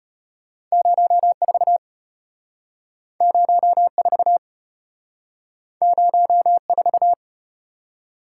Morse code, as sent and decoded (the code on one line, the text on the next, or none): ----- ....- / ----- ....- / ----- ....-
04 04 04